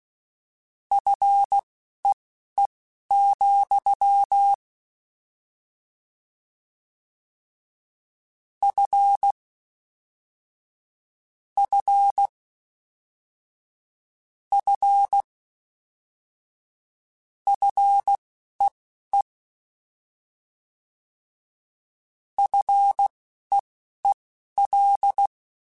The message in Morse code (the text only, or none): FEE, F F F FEE FEEL